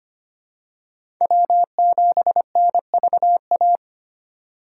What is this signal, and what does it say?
Morse code, heard as a keyed tone, about 25 wpm, 700 Hz. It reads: W7NVA
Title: USA Call Signs - 4x Character Spacing - 25wpm